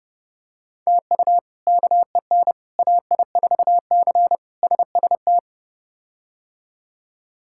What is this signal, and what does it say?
Morse code, heard as a keyed tone, about 32 wpm, 700 Hz. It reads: TU KEN AI4C SST
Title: K1USN SST QSO Practice - 30wpm